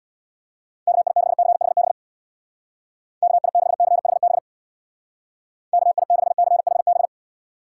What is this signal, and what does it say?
Morse code, heard as a keyed tone, about 65 wpm, 700 Hz. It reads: KI6CHB KI6CHB KI6CHB